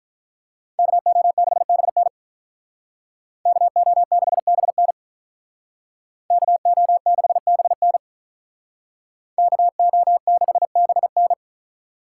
Morse code, heard as a keyed tone, about 43 wpm, 700 Hz. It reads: KO6BN KO6BN KO6BN KO6BN
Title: Speed Racing - USA 2x2 Call Signs 35wpm